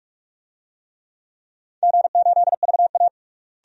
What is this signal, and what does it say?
Morse code, heard as a keyed tone, about 46 wpm, 700 Hz. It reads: G8VA
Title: DXCC-100 Call Signs - 4 Characters - Repeated 3x - 45wpm